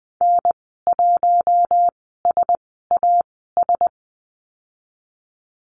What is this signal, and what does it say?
Morse code, heard as a keyed tone, about 20 wpm, spaced as for 14 wpm, 700 Hz. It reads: N1SAS
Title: 088 Lesson 15g - Number 5 - Call Signs 20wpm